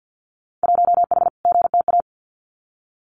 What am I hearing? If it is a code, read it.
Morse code, heard as a keyed tone, about 50 wpm, 700 Hz. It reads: 1H GTA
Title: Winter Field Day Exchanges Encore 50wpm